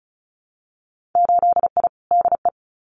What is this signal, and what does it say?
Morse code, heard as a keyed tone, about 36 wpm, 700 Hz. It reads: 8I DE
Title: Winter Field Day Exchanges Encore 35wpm